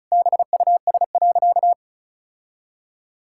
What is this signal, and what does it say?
Morse code, heard as a keyed tone, about 36 wpm, 700 Hz. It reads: BUS.
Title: Sentences from Top 1000 Words - Review Part 6 - 35wpm